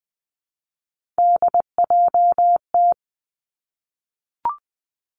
Morse code, heard as a keyed tone, about 19 wpm, 700 Hz. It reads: DJT
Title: Three Challenging Letter-Number Combinations 20wpm